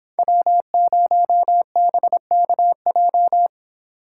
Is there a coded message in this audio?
Yes: Morse code from 0.2 to 3.5 s, heard as a keyed tone, about 29 wpm, 700 Hz. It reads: W0BKJ